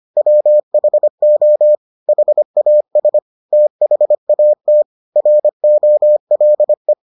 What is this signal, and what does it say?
Morse code, heard as a keyed tone, about 25 wpm, 600 Hz. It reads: WHO HAS THAT ROLE